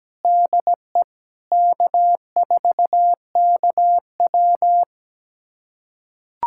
Morse code, heard as a keyed tone, about 17 wpm, 700 Hz. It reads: DE K4KW
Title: USA Callsign DE Callsign - 4 Characters - 17wpm